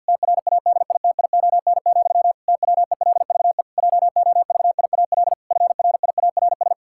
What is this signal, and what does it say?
Morse code, heard as a keyed tone, about 52 wpm, 700 Hz. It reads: TRADITION, TWELVE JOVIAL FRIARS
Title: Pangram Challenge v2 - 50wpm